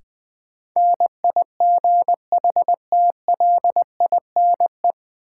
Morse code, heard as a keyed tone, about 20 wpm, 700 Hz. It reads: NIGHTLINE
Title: Sets of 5 Words - New Word Drill - 20wpm